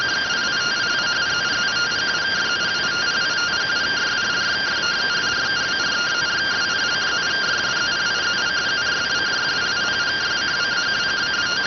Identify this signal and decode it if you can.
Two-tone teleprinter signal, about 170 Hz shift.